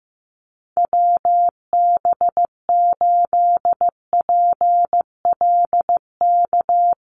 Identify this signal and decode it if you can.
Morse code, heard as a keyed tone, about 15 wpm, 700 Hz. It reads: WB8PLK
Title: USA Call Signs Repeated 2x 15wpm